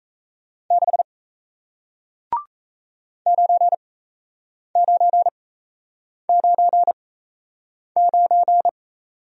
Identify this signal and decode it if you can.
Morse code, heard as a keyed tone, about 38 wpm, spaced as for 8 wpm, 700 Hz. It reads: 6 9999